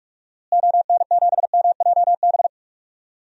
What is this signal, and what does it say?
Morse code, heard as a keyed tone, about 41 wpm, 700 Hz. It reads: ON7MJB
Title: DXCC-100 Call Signs - 6 Characters - Repeated 3x - 45wpm